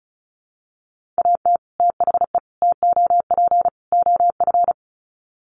Morse code, heard as a keyed tone, about 34 wpm, 700 Hz. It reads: AT THE TOP OF
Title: Sets of 4 Words 35wpm